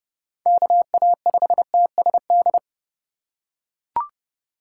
Morse code, heard as a keyed tone, about 27 wpm, 700 Hz. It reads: KA5TSD